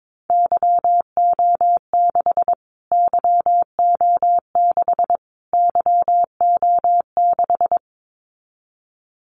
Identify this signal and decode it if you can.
Morse code, heard as a keyed tone, about 22 wpm, 700 Hz. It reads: YO6 YO6 YO6
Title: Callsign Prefixes for Top 25 DXCC - Repeated 3x - 22wpm